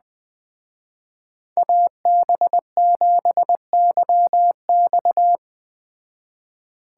Morse code, heard as a keyed tone, about 20 wpm, 700 Hz. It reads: AB7YX